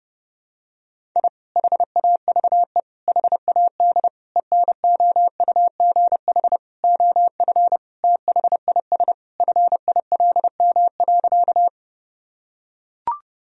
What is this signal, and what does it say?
Morse code, heard as a keyed tone, about 30 wpm, 700 Hz. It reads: I HAVE HAD ENOUGH OF THIS FILM.
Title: Sentences from Top 1000 Words - Review Part 3 - 30wpm